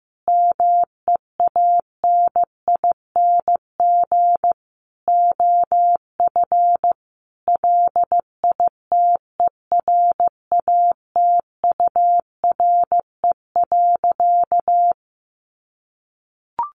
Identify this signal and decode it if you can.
Morse code, heard as a keyed tone, about 15 wpm, 700 Hz. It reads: MEANING OF LITERATURE.
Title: Sentences from Top 900 Words 15wpm